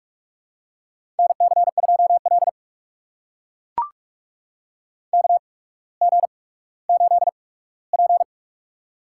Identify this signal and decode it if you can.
Morse code, heard as a keyed tone, about 44 wpm, spaced as for 19 wpm, 700 Hz. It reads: NC2L KG8P
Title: USA Call Signs - 4 Characters - 8x Character Spacing - 45wpm